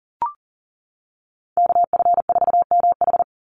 Morse code, heard as a keyed tone, about 40 wpm, 700 Hz. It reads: KF4MH
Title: NA SOTA Chaser Callsigns 40wpm